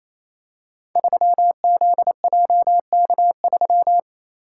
Morse code, heard as a keyed tone, about 28 wpm, 700 Hz. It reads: 3ZJK3
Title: Code Groups - 5 Letter:Numbers - 28wpm